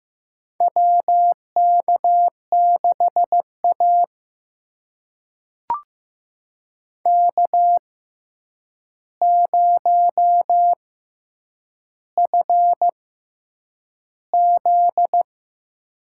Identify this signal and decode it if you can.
Morse code, heard as a keyed tone, about 15 wpm, spaced as for 8 wpm, 700 Hz. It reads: WK6A K0FZ
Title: USA Call Signs - 4 Characters - 6x Character Spacing - 15wpm